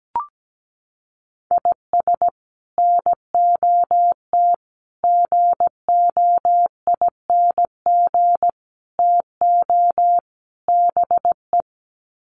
Morse code, heard as a keyed tone, about 17 wpm, 700 Hz. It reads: IS NOT GOING TO BE
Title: Sets of 5 Words 17wpm